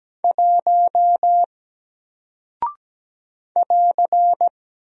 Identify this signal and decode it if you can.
Morse code, heard as a keyed tone, about 17 wpm, 700 Hz. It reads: 1 <AR>
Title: Single Character (Letter:Number:Punctuation:Prosign) Alt 17wpm